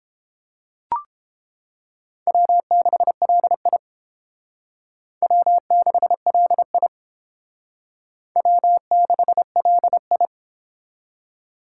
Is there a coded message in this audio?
Yes: Morse code from 2.3 to 10.3 s, heard as a keyed tone, about 29 wpm, 700 Hz. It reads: W6LI W6LI W6LI